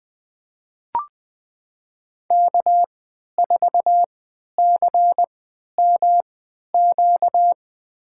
Morse code, heard as a keyed tone, about 20 wpm, spaced as for 11 wpm, 700 Hz. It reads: K4CMQ
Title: USA Call Signs - 5 Characters - 3x Character Spacing - 20wpm